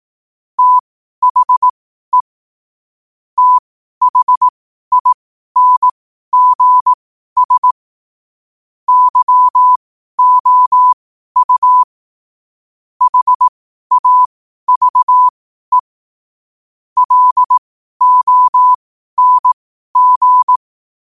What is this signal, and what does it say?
Morse code, heard as a keyed tone, about 18 wpm, spaced as for 12 wpm, 1000 Hz. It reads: THE THINGS YOU HAVE LONG
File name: qotdfiles / 12wpm